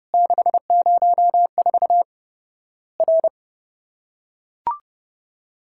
Morse code, heard as a keyed tone, about 30 wpm, 700 Hz. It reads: 604 R
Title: Worked All Europe DX Contest - QTC Exchange - 30wpm